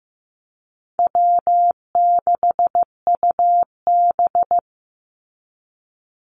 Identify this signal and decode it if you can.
Morse code, heard as a keyed tone, about 15 wpm, 700 Hz. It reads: W6UB
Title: NA SOTA Chaser Callsigns 15wpm